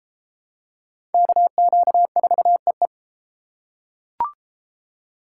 Morse code, heard as a keyed tone, about 31 wpm, 700 Hz. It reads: KQ4EE